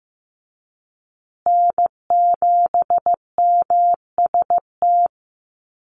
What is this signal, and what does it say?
Morse code, heard as a keyed tone, about 15 wpm, 700 Hz. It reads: N7MST
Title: USA Call Signs Repeated 2x 15wpm